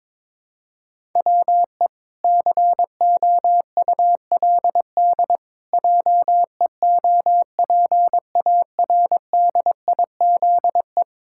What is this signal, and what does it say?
Morse code, heard as a keyed tone, about 22 wpm, 700 Hz. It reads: WE COULD JEOPARDIZE
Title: Pangram Challenge v2 - 22wpm